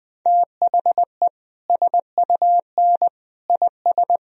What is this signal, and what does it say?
Morse code, heard as a keyed tone, about 21 wpm, 700 Hz. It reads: THE SUN IS
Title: Sentences from Top 1000 Words - Review Part 1 - 20wpm